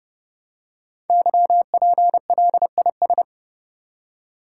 Morse code, heard as a keyed tone, about 30 wpm, 700 Hz. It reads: YPLIS
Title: Code Groups - 5 Letters - 30wpm